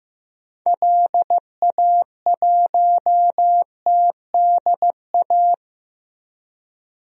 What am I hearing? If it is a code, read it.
Morse code, heard as a keyed tone, about 15 wpm, 700 Hz. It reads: LA1TDA
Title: DXCC-100 Call Signs - 6 Characters - Repeated 2x - 15wpm